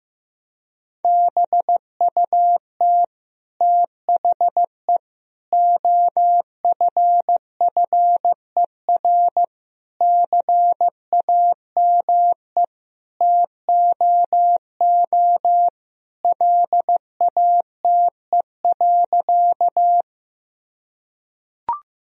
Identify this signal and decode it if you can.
Morse code, heard as a keyed tone, about 15 wpm, 700 Hz. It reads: BUT THE OFFER CAME TOO LATE.
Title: Sentences from Top 1000 Words - Review Part 4 - 15wpm